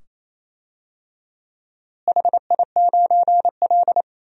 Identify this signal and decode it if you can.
Morse code, heard as a keyed tone, about 28 wpm, 700 Hz. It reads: HI9L
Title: DXCC-100 Call Signs - 4 Characters - Repeated 2x - 28wpm